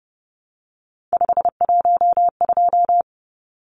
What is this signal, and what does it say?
Morse code, heard as a keyed tone, about 30 wpm, 700 Hz. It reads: 512
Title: Numbers - 3-Digit 30wpm